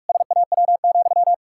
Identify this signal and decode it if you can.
Morse code, heard as a keyed tone, about 44 wpm, 700 Hz. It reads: SAW,
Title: Pangram Challenge v2 - 45wpm